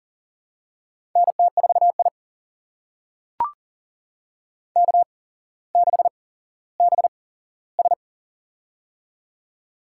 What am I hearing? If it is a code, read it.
Morse code, heard as a keyed tone, about 40 wpm, spaced as for 17 wpm, 700 Hz. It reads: NT4I K6BS
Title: USA Call Signs - 4 Characters - 8x Character Spacing - 40wpm